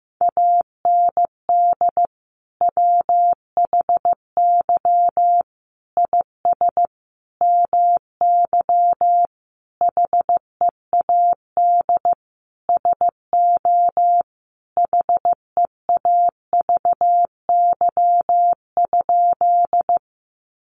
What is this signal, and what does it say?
Morse code, heard as a keyed tone, about 15 wpm, 700 Hz. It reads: AND WHY IS MY HEAD SO HEAVY?
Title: Sentences from Top 1100 Words 15wpm